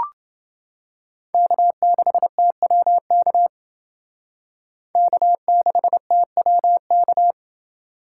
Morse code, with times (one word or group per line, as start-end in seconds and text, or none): 1.3-3.5 K6TWK
5.0-7.3 K6TWK